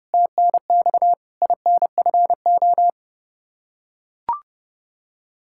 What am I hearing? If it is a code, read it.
Morse code, heard as a keyed tone, about 32 wpm, 700 Hz. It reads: TNX INFO
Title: K1USN SST QSO Practice 30wpm